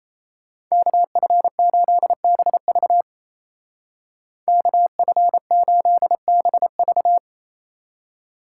Morse code, heard as a keyed tone, about 33 wpm, 700 Hz. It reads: KF8BV KF8BV
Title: Speed Racing - USA 2x2 Call Signs 28wpm